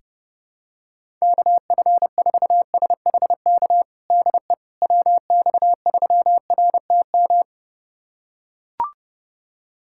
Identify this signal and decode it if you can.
Morse code, heard as a keyed tone, about 32 wpm, 700 Hz. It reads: KF4SHK DE WX3RTM
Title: USA Callsign DE Callsign - 6 Characters - 30wpm